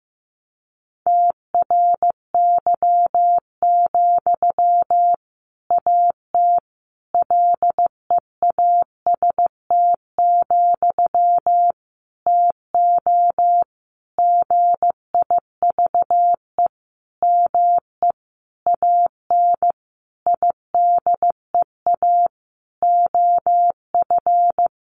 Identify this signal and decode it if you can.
Morse code, heard as a keyed tone, about 15 wpm, 700 Hz. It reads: TRY, AT LEAST, TO GIVE ME AN IDEA OF
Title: Sentences from Top 700 Words 15wpm